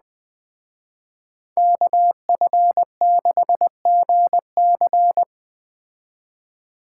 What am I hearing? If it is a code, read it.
Morse code, heard as a keyed tone, about 20 wpm, 700 Hz. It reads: KF6GC